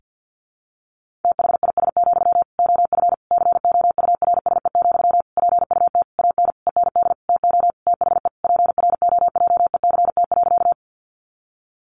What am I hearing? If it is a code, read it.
Morse code, heard as a keyed tone, about 50 wpm, 700 Hz. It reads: THIS, OF COURSE, PUT AN END TO THE PROJECT.
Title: Sentences from Top 1400 Words 50wpm